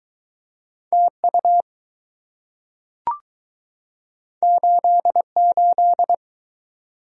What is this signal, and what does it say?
Morse code, heard as a keyed tone, about 20 wpm, 700 Hz. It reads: TU 88